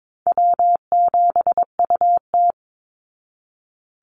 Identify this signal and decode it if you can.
Morse code, heard as a keyed tone, about 22 wpm, 690 Hz. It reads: W7UT
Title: Call Signs Repeated 3x - 22wpm